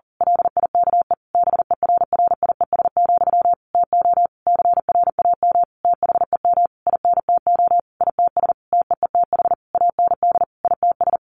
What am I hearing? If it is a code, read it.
Morse code, heard as a keyed tone, about 40 wpm, 700 Hz. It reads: LIKE BERRIES, TO CRAM THEM INTO ITS TEETH AND ITS
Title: A Sound Of Thunder 40wpm